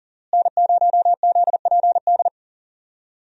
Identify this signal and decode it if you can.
Morse code, heard as a keyed tone, about 37 wpm, 700 Hz. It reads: N0ZPD
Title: USA Call Signs - 5 Characters - Repeated 3x - 40wpm